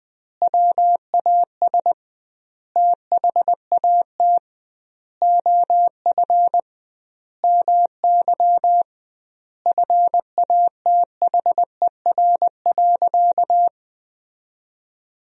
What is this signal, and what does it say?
Morse code, heard as a keyed tone, about 20 wpm, 700 Hz. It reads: WAS THAT OF MY FATHER.
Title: 2x Word-Spacing - Sentences from Top 500 Words - 20wpm